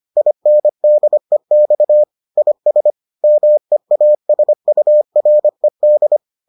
Morse code, heard as a keyed tone, about 25 wpm, 600 Hz. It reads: INDEX IS MEASURED